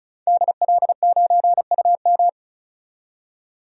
Morse code, heard as a keyed tone, about 34 wpm, 700 Hz. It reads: DL9UM
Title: DXCC-100 Call Signs - 5 Characters - 35wpm